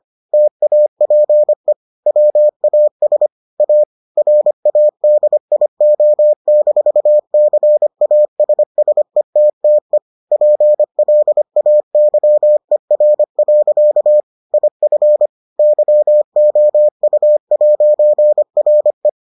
Morse code, heard as a keyed tone, about 25 wpm, 600 Hz. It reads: TAPE WAS A RADIO-CASSETTE PLAYER. IF YOU'RE